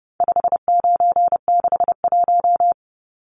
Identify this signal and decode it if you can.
Morse code, heard as a keyed tone, about 30 wpm, 700 Hz. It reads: 5961
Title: Numbers - 4-Digit 30wpm